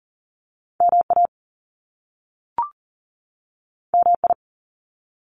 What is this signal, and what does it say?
Morse code, heard as a keyed tone, about 42 wpm, 700 Hz. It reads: MA MI
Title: US States 40wpm